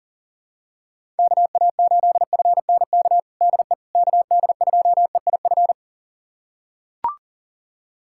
Morse code, heard as a keyed tone, about 40 wpm, 700 Hz. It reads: KA8FNK DE KD2EIF
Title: USA Callsign DE Callsign - 6 Characters - 40wpm